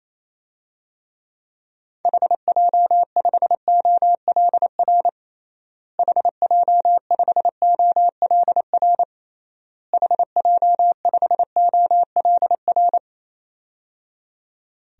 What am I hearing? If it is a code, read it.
Morse code, heard as a keyed tone, about 28 wpm, 700 Hz. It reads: HJ5OLR HJ5OLR HJ5OLR
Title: DXCC-100 Call Signs - 6 Characters - Repeated 3x - 28wpm